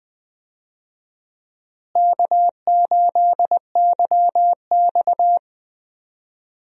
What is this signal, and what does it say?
Morse code, heard as a keyed tone, about 20 wpm, 700 Hz. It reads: K8YX
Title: USA Call Signs - 4 Characters - Repeated 2x - 20wpm